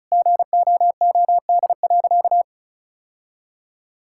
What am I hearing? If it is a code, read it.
Morse code, heard as a keyed tone, about 34 wpm, 700 Hz. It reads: GOOD.
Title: Sentences from Top 1000 Words - Review Part 5 - 35wpm